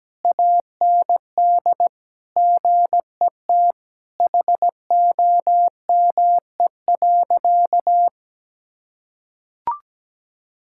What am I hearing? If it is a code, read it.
Morse code, heard as a keyed tone, about 17 wpm, 700 Hz. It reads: AND GET HOME.
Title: Sentences from Top 1600 Words 17wpm